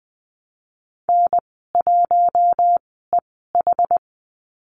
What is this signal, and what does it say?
Morse code, heard as a keyed tone, about 20 wpm, spaced as for 15 wpm, 700 Hz. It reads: N1EH
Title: 082 Lesson 14g - Number 2 - Call Signs 20wpm